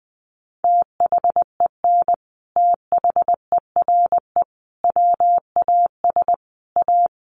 Morse code, heard as a keyed tone, about 20 wpm, 700 Hz. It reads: THEN THERE WAS A
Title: Sentences from Top 1100 Words 20wpm